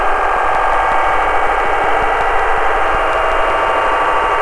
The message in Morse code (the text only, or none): none